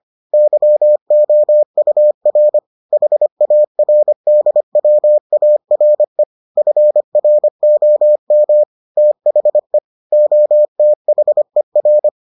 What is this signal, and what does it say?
Morse code, heard as a keyed tone, about 25 wpm, 600 Hz. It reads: YOUR HARDWARE FROM THE OTHER